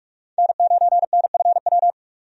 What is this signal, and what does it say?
Morse code, heard as a keyed tone, about 45 wpm, 700 Hz. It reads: N9NFW
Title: USA Call Signs - 5 Characters - 45wpm